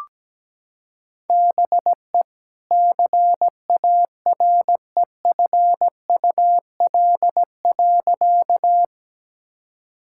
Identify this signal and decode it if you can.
Morse code, heard as a keyed tone, about 17 wpm, 700 Hz. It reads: BE CAREFUL.
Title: Sentences from Top 1600 Words 17wpm